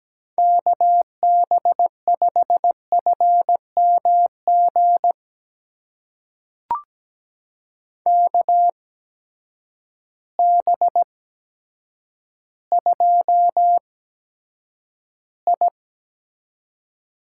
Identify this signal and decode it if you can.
Morse code, heard as a keyed tone, about 17 wpm, spaced as for 8 wpm, 700 Hz. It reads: KB5FMG KB2I
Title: USA Call Signs - 6 Characters - 8x Character Spacing - 17wpm